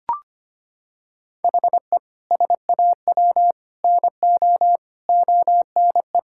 Morse code, heard as a keyed tone, about 25 wpm, 700 Hz. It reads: HE SAW NO ONE
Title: Sentences from Top 1000 Words - Review Part 6 - 25wpm